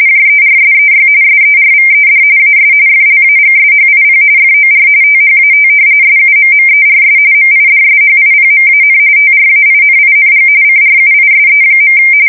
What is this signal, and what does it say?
RTTY, about 45.45 baud. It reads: JERSEY SERVED AS FLAGSHIP FOR THAT SUMMERS MIDSHIPMAN TRAINING CRUISE.
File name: fsk170 / battleshipnj